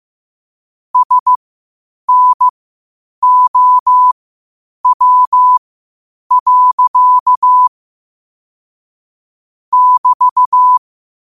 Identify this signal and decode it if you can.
Morse code, heard as a keyed tone, about 15 wpm, spaced as for 8 wpm, 1000 Hz. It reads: SNOW. <BT>
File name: qotdfiles / 8wpm